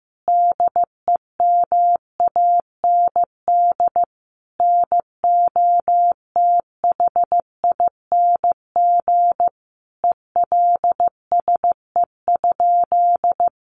Morse code, heard as a keyed tone, about 15 wpm, 700 Hz. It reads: DEMAND NOTHING ELSE?
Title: Sentences from Top 800 Words 15wpm